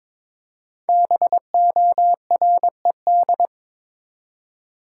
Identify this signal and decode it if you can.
Morse code, heard as a keyed tone, about 22 wpm, 690 Hz. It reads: BORED
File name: 5 Letter Words - First 3 Letters Given - 22wpm